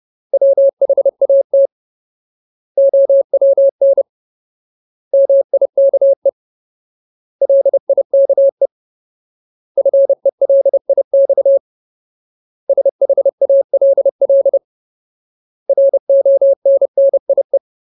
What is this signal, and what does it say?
Morse code, heard as a keyed tone, about 30 wpm, 550 Hz. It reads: WHAT OWN MIKE LIKE FELIX SHALL RONNIE
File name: Ultimate Word List / 30WPM Ultimate